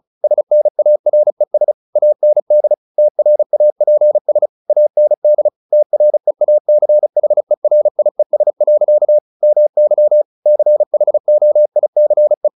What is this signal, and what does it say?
Morse code, heard as a keyed tone, about 34 wpm, 600 Hz. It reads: SNARES AND TRAPS AND TREACHERIES. MY CHOICE
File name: the old man and the sea / the old man and the sea 1 35wpm